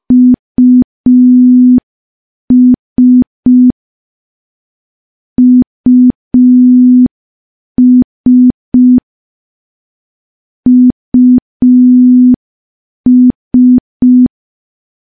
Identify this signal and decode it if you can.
Morse code, heard as a keyed tone, about 5 wpm, 250 Hz. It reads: US US US
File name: morse / morse-training-05WPM